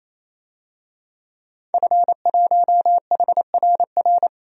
Morse code, heard as a keyed tone, about 28 wpm, 700 Hz. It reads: F1HRR
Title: DXCC-100 Call Signs - 5 Characters - Repeated 3x - 28wpm